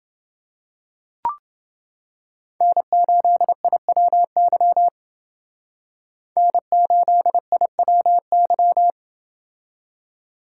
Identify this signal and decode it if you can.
Morse code, heard as a keyed tone, about 28 wpm, 700 Hz. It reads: N8IWY N8IWY